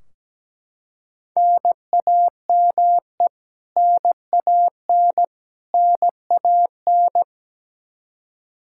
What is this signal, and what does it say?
Morse code, heard as a keyed tone, about 17 wpm, 700 Hz. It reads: NAME NAN NAN
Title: Practice QSO - Exchange 1 - Beginning - 17wpm